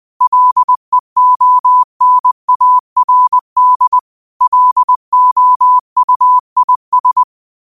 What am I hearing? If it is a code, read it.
Morse code, heard as a keyed tone, about 20 wpm, 1000 Hz. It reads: LEONARD LOUIS
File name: qotdfiles / 20wpm